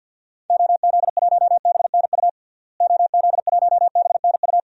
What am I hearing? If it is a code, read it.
Morse code, heard as a keyed tone, about 48 wpm, 700 Hz. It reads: OZ1BNU OZ1BNU
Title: DXCC-100 Call Signs - 6 Characters - Repeated 2x - 50wpm